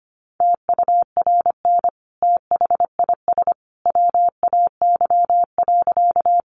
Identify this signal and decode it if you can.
Morse code, heard as a keyed tone, about 25 wpm, 700 Hz. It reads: TURN THIS WAY.
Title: Sentences from Top 200 Words 25wpm